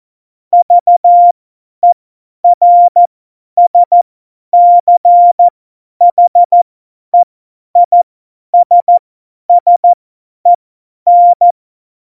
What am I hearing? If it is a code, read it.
Morse code, heard as a keyed tone, about 14 wpm, spaced as for 10 wpm, 700 Hz. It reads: VERSCHEISSEN
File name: Morsekurs / Witze / Witz 07. Mai 2019, 50 F70